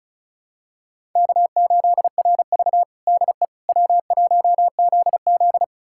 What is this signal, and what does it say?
Morse code, heard as a keyed tone, about 36 wpm, 700 Hz. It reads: K8RV DE W1ZZ
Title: USA Callsign DE Callsign - 4 Characters - 35wpm